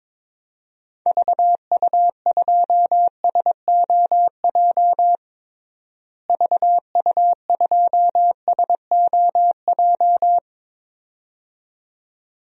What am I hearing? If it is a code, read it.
Morse code, heard as a keyed tone, about 22 wpm, 700 Hz. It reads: VU2SOJ VU2SOJ
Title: DXCC-100 Call Signs - 6 Characters - Repeated 2x - 22wpm